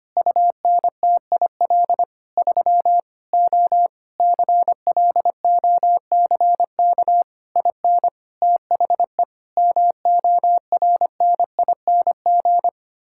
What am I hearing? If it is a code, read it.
Morse code, heard as a keyed tone, about 25 wpm, 700 Hz. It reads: UNTIL 3 O CLOCK IN THE MORNING
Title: Pangram Challenge v2 - 25wpm